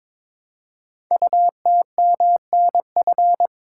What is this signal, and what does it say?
Morse code, heard as a keyed tone, about 22 wpm, 700 Hz. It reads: UTMNF
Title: Code Groups - 5 Letters - 22wpm